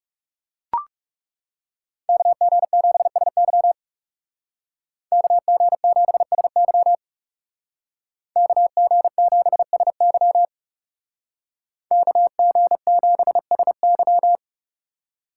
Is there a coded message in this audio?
Yes: Morse code from 2.1 to 14.4 s, heard as a keyed tone, about 39 wpm, 700 Hz. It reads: KG7SY KG7SY KG7SY KG7SY